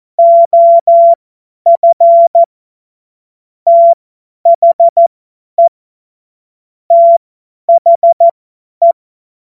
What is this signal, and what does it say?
Morse code, heard as a keyed tone, about 14 wpm, spaced as for 10 wpm, 680 Hz. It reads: OF THE THE
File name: Morsekurs / Witze / Witz 14. Dezember 2021, 50 F70